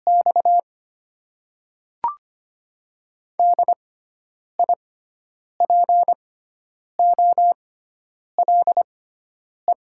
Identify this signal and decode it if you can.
Morse code, heard as a keyed tone, about 25 wpm, spaced as for 9 wpm, 700 Hz. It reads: X DIPOLE
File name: Top 90 QSO Elements - 6x Character Spacing - 25wpm